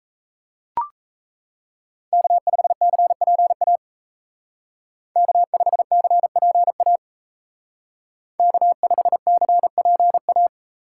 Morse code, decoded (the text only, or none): K5CPA K5CPA K5CPA